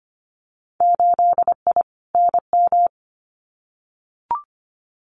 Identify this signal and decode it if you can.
Morse code, heard as a keyed tone, about 25 wpm, 700 Hz. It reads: 8I NM
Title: Winter Field Day Exchanges Encore 25wpm